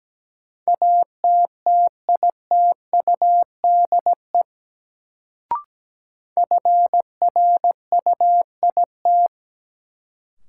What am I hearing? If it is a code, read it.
Morse code, heard as a keyed tone, about 17 wpm, 700 Hz. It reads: ATTITUDE FRUIT
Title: Sentences from Top 1300 Words - New Word Drill - 17wpm